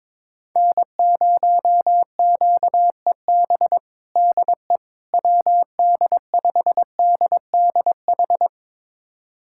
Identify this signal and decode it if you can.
Morse code, heard as a keyed tone, about 22 wpm, 700 Hz. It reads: N0QEB DE WD5DDH
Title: USA Callsign DE Callsign 22wpm